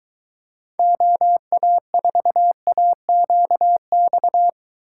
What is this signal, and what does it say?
Morse code, heard as a keyed tone, about 24 wpm, 700 Hz. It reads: OA4AQX